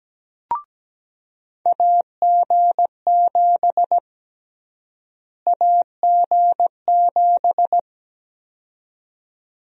Morse code, heard as a keyed tone, about 17 wpm, 700 Hz. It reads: AG7 AG7
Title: USA Call Sign Prefixes - Repeated 2x - 17wpm